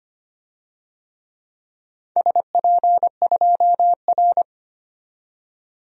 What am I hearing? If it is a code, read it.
Morse code, heard as a keyed tone, about 25 wpm, 700 Hz. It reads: SP2R
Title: DXCC-100 Call Signs - 4 Characters - 25wpm